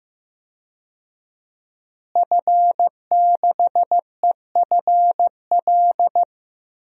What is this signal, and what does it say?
Morse code, heard as a keyed tone, about 15 wpm, 700 Hz. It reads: F6EFL
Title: DXCC-100 Call Signs - 5 Characters - Repeated 2x - 15wpm